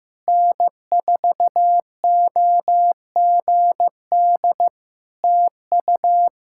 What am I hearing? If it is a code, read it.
Morse code, heard as a keyed tone, about 15 wpm, 700 Hz. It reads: N4OGD TU
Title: Practice QSO - Exchange 4 - Ending - 15wpm